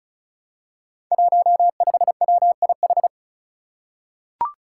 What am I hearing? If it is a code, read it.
Morse code, heard as a keyed tone, about 36 wpm, 700 Hz. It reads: 15WIH
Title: Five Challenging Letter-Number Combinations 35wpm